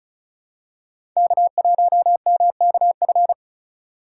Morse code, heard as a keyed tone, about 34 wpm, 700 Hz. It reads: K1MKF
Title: USA Call Signs - 5 Characters - 6x Character Spacing - 35wpm